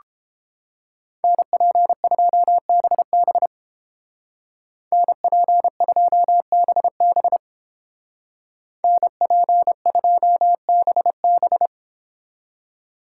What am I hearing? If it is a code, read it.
Morse code, heard as a keyed tone, about 29 wpm, 700 Hz. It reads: NP2BB NP2BB NP2BB